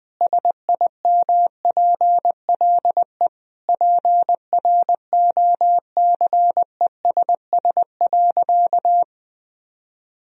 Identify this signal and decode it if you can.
Morse code, heard as a keyed tone, about 21 wpm, 700 Hz. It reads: SIMPLE PROCESS.
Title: Sentences from Top 1000 Words - Review Part 3 - 20wpm